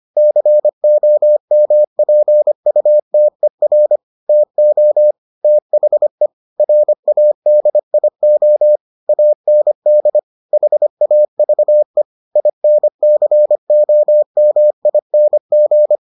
Morse code, heard as a keyed tone, about 25 wpm, 600 Hz. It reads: COMPUTER TO THE RADIO AND HAVE INCOMING